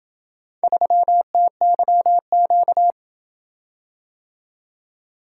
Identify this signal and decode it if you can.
Morse code, heard as a keyed tone, about 27 wpm, 700 Hz. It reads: 3TYQ